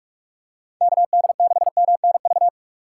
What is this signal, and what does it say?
Morse code, heard as a keyed tone, about 47 wpm, 700 Hz. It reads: KD6MNV
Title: USA Call Signs - 6 Characters - 3x Character Spacing - 45wpm